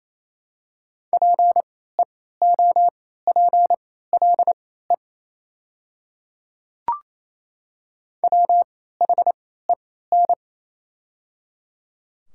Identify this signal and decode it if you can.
Morse code, heard as a keyed tone, about 28 wpm, spaced as for 16 wpm, 690 Hz. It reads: PEOPLE WHEN
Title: 3x Character Spacing - Top 100 Words - 28wpm